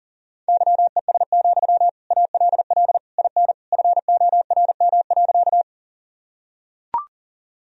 Morse code, heard as a keyed tone, about 40 wpm, 700 Hz. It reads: YES, ALL IN FORM.
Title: Sentences from Top 1000 Words - Review Part 1 - 40wpm